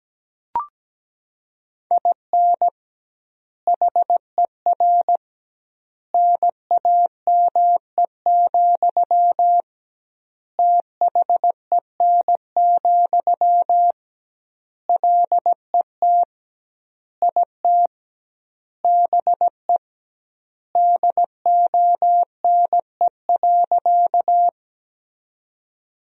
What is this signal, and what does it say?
Morse code, heard as a keyed tone, about 17 wpm, 700 Hz. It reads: IN HER NAME, THEN, LET IT BE DONE.
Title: 2x Word-Spacing - Sentences from Top 500 Words - 17wpm